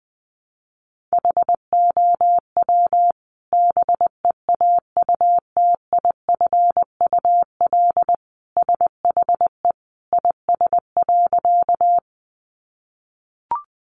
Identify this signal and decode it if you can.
Morse code, heard as a keyed tone, about 20 wpm, 700 Hz. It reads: HOW BEAUTIFUL SHE IS.
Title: Sentences from Top 900 Words 20wpm